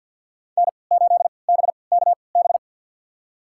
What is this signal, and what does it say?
Morse code, heard as a keyed tone, about 53 wpm, spaced as for 29 wpm, 700 Hz. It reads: N8BKB
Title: USA Call Signs - 3x Character Spacing - 50wpm